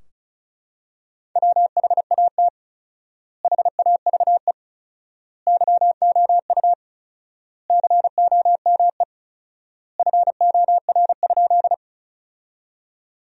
Morse code, heard as a keyed tone, about 36 wpm, 700 Hz. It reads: WHAT HAVE YOU COME FOR?
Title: 4x Word-Spacing - Sentences from Top 100 Words - 35wpm